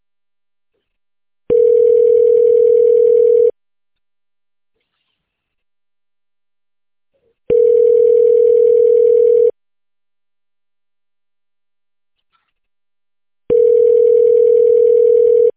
Ringback tone (North America), 440 + 480 Hz.